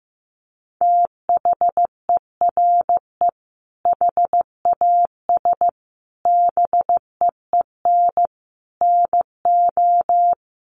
Morse code, heard as a keyed tone, about 15 wpm, 700 Hz. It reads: THERE HAS BEEN NO
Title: Sentences from Top 700 Words 15wpm